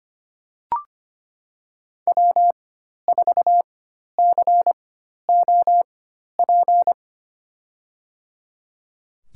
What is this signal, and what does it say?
Morse code, heard as a keyed tone, about 25 wpm, spaced as for 12 wpm, 700 Hz. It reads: W4COP
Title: USA Call Signs - 5 Characters - 4x Character Spacing - 25wpm